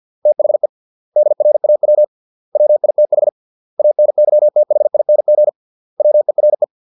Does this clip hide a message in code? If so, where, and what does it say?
Yes: Morse code from 0.3 to 6.6 s, heard as a keyed tone, about 50 wpm, 600 Hz. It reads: THE DRAW WITH ANYTHING WERE